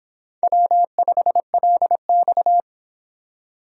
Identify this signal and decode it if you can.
Morse code, heard as a keyed tone, about 25 wpm, 700 Hz. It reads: W5LX